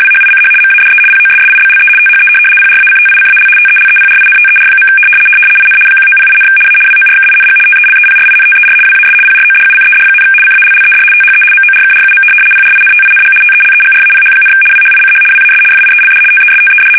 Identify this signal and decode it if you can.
RTTY, about 75 baud. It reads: WHICH MADE POSSIBLE THE PRACTICAL TELEPRINTER. IN 1908, A WORKING TELEPRINTER WAS PRODUCED, CALLED THE MORKRUM PRINTING TELEGRAPH, WHICH WAS FIELD TESTED
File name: fsk850-75B / teletype-corp